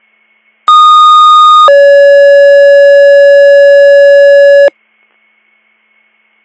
Two-tone page, heard as a one-second tone, then a three-second tone, about 1230 then 560 Hz. It is heard from 0.7 to 4.7 s.